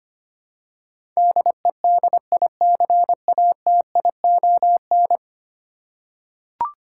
Morse code, heard as a keyed tone, about 25 wpm, 700 Hz. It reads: DEDICATION
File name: Ten Letter Words - 25wpm